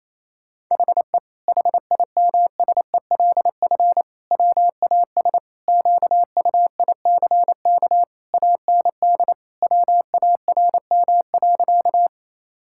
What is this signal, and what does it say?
Morse code, heard as a keyed tone, about 28 wpm, 700 Hz. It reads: HE HIMSELF WAS QUICK AND WARM.
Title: Sentences from Top 1600 Words 28wpm